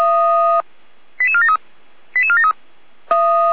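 Five-tone selective call (ZVEI), heard as quick runs of five tones, tones of about 70 ms. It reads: 88473 88473
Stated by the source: SDR-received ZVEI five-tone selective call.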